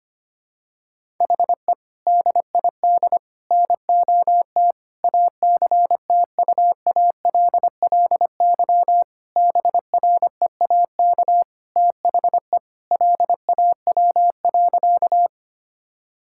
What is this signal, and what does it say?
Morse code, heard as a keyed tone, about 25 wpm, 700 Hz. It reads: HE DID NOT ACTUALLY BREAK THE LAW.
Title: Sentences from Top 1000 Words - Review Part 6 - 25wpm